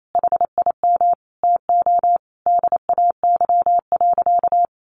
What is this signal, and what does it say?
Morse code, heard as a keyed tone, about 28 wpm, 700 Hz. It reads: HIM TO DAY.
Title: Sentences from Top 1100 Words 28wpm